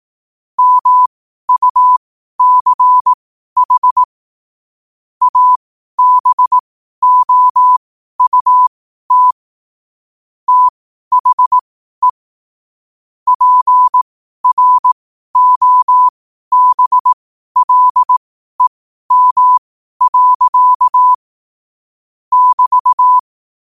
Morse code, heard as a keyed tone, about 18 wpm, spaced as for 12 wpm, 1000 Hz. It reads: MUCH ABOUT THE PROBLEM. <BT>
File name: qotdfiles / 12wpm